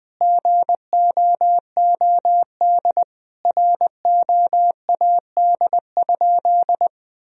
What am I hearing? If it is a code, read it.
Morse code, heard as a keyed tone, about 20 wpm, 700 Hz. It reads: GOOD ROAD?
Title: Sentences from Top 1000 Words - Review Part 2 - 20wpm